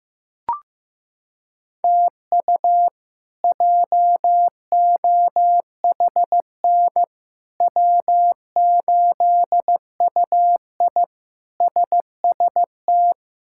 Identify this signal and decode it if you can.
Morse code, heard as a keyed tone, about 15 wpm, 700 Hz. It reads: TU JOHN W8UI SST
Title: K1USN SST QSO Practice - 15wpm